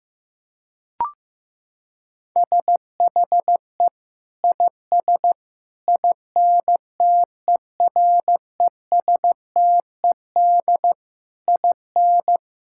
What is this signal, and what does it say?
Morse code, heard as a keyed tone, about 15 wpm, 700 Hz. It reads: SHE IS INTERESTED IN
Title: Sentences from Top 1000 Words - Review Part 6 - 15wpm